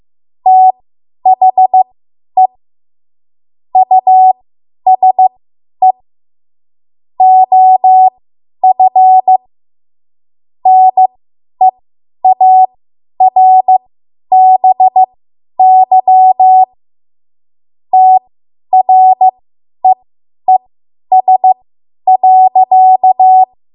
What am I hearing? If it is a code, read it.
Morse code, heard as a keyed tone, about 15 wpm, spaced as for 10 wpm, 750 Hz. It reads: THE USE OF NEARBY TREES.
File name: Archive / 10 WPM / 10WPM